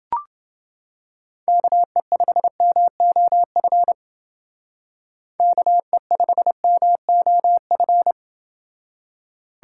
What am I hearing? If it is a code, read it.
Morse code, heard as a keyed tone, about 29 wpm, 700 Hz. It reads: KE5MOF KE5MOF